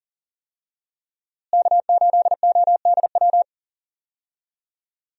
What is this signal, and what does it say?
Morse code, heard as a keyed tone, about 40 wpm, 700 Hz. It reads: K8ODW
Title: USA Call Signs - 5 Characters - Repeated 3x - 40wpm